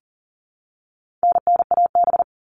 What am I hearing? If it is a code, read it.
Morse code, heard as a keyed tone, about 40 wpm, 700 Hz. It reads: NNAB
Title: All Cut Numbers - 4-Digit 40wpm